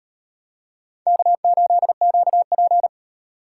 Morse code, heard as a keyed tone, about 32 wpm, 700 Hz. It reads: K8QP
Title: Speed Racing - USA 1x2 Call Signs 28wpm